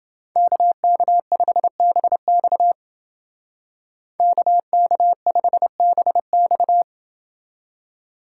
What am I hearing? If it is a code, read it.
Morse code, heard as a keyed tone, about 29 wpm, 700 Hz. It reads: KK5BX KK5BX